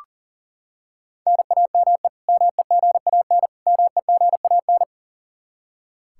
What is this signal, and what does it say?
Morse code, heard as a keyed tone, about 40 wpm, 700 Hz. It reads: NAME MEGAN MEGAN
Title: Practice QSO - Exchange 1 - Beginning - 40wpm